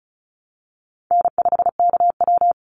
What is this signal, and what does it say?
Morse code, heard as a keyed tone, about 34 wpm, 700 Hz. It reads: N5KW
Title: Easy Call Signs Repeated 2x - 35wpm